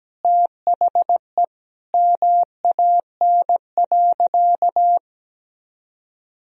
Morse code, heard as a keyed tone, about 17 wpm, 700 Hz. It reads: THE MAN.
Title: Sentences from Top 1000 Words - Review Part 3 - 17wpm